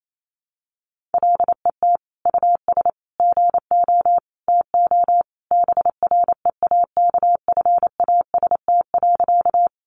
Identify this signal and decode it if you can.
Morse code, heard as a keyed tone, about 28 wpm, 700 Hz. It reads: LET US GO TO BREAKFAST.
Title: Sentences from Top 1500 Words 28wpm